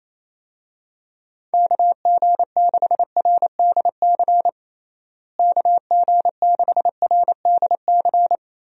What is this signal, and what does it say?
Morse code, heard as a keyed tone, about 27 wpm, 700 Hz. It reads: KG6RDC KG6RDC
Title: USA Call Signs - 6 Characters - Repeated 2x - 28wpm